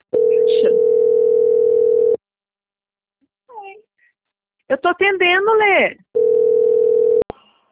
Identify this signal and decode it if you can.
Ringback tone (North America), 440 + 480 Hz, with speech.
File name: out- - - - - .15555